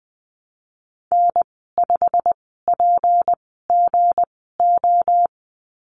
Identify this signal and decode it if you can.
Morse code, heard as a keyed tone, about 19 wpm, spaced as for 14 wpm, 700 Hz. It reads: N5PGO
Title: 207 Lesson 33g - Letter j - Call Signs 20wpm